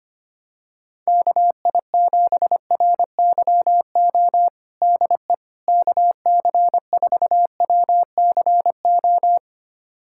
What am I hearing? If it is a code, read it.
Morse code, heard as a keyed tone, about 25 wpm, 700 Hz. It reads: KI7RYO DE KC4WCO
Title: USA Callsign DE Callsign - 6 Characters - 25wpm